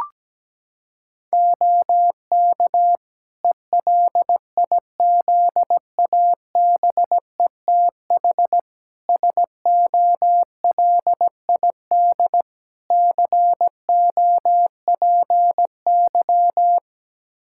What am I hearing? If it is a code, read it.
Morse code, heard as a keyed tone, about 17 wpm, 700 Hz. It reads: OK ELIZABETH SOLID COPY
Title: Practice QSO - Exchange 3 - More Info - 17wpm